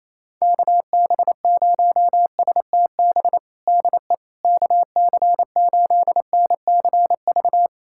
Morse code, heard as a keyed tone, about 27 wpm, 700 Hz. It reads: KB0STB DE KC8NCV
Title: USA Callsign DE Callsign - 6 Characters - 28wpm